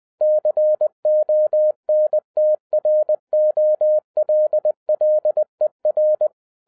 Morse code, heard as a keyed tone, about 20 wpm, 610 Hz. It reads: CONTROLLER
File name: 2012-01-mp3 / n16-THROUGH EIGHT MONTHS-20wpm